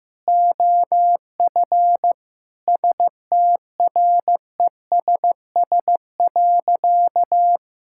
Morse code, heard as a keyed tone, about 15 wpm, 700 Hz. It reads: OF STRESS.
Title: Sentences from Top 1000 Words - Review Part 6 - 15wpm